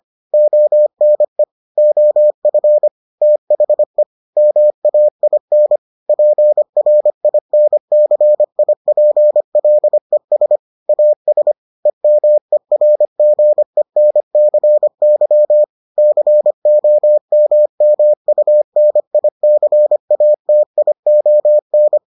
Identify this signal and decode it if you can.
Morse code, heard as a keyed tone, about 25 wpm, 600 Hz. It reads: ONE OF THE MAIN PRINCIPLES AS EMERGENCY COMMUNICATION